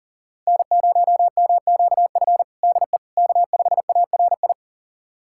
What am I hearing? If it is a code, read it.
Morse code, heard as a keyed tone, about 40 wpm, 700 Hz. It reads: N0MQF DE K5ARI
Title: USA Callsign DE Callsign 40wpm